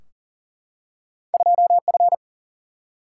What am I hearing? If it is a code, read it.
Morse code, heard as a keyed tone, about 43 wpm, 700 Hz. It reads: 2 F
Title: Single-Letter Single-Number 40wpm